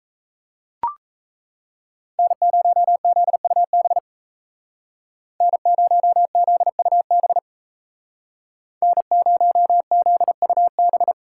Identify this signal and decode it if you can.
Morse code, heard as a keyed tone, about 36 wpm, 700 Hz. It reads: N0ZUB N0ZUB N0ZUB